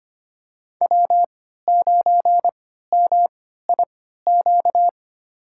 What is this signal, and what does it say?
Morse code, heard as a keyed tone, about 25 wpm, spaced as for 14 wpm, 700 Hz. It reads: W9MIQ
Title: USA Call Signs - 5 Characters - 3x Character Spacing - 25wpm